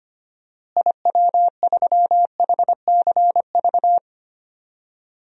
Morse code, heard as a keyed tone, about 25 wpm, 700 Hz. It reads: IW3HCV
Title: DXCC-100 Call Signs - 6 Characters - Repeated 3x - 25wpm